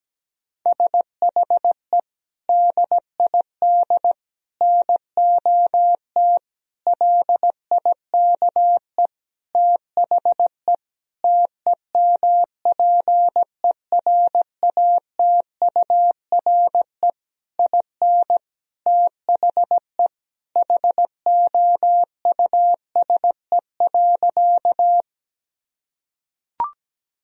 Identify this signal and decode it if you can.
Morse code, heard as a keyed tone, about 17 wpm, 700 Hz. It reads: SHE DID NOT LIKE THE TEMPERATURE IN THE HOUSE.
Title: Sentences from Top 1000 Words - Review Part 4 - 17wpm